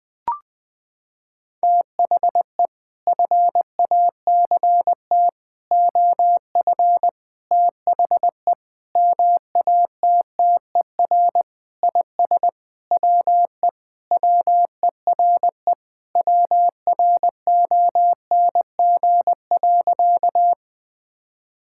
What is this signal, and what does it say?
Morse code, heard as a keyed tone, about 20 wpm, 700 Hz. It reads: THE FACT OF THE MATTER IS WE WERE WRONG.
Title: Sentences with 5-Word Phrases - 20wpm